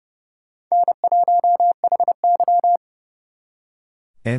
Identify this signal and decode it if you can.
Morse code, heard as a keyed tone, about 38 wpm, 700 Hz. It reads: N1HY
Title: Speed Racing - USA 1x2 Call Signs 30wpm